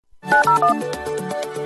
DTMF, heard as telephone touch tones, tones of about 120 ms. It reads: 901